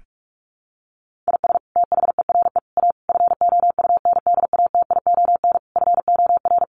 Morse code, heard as a keyed tone, about 44 wpm, 700 Hz. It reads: IS THERE A FOUNDATION FOR
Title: Sentences from Top 1200 Words 45wpm